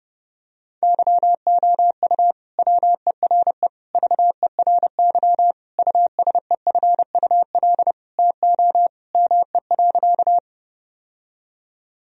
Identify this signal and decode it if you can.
Morse code, heard as a keyed tone, about 31 wpm, 700 Hz. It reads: YOU WERE VERY USEFUL TO ME.
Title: Sentences from Top 1000 Words - Review Part 4 - 30wpm